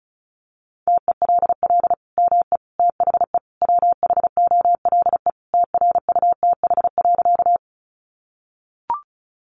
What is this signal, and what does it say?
Morse code, heard as a keyed tone, about 34 wpm, 700 Hz. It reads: TELL ME THE WHOLE TRUTH.
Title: Sentences from Top 800 Words 35wpm